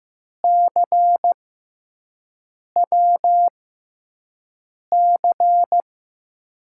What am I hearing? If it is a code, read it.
Morse code, heard as a keyed tone, about 15 wpm, spaced as for 5 wpm, 700 Hz. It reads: CWC
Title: USA Call Signs - 6 Characters - 6x Character Spacing - 15wpm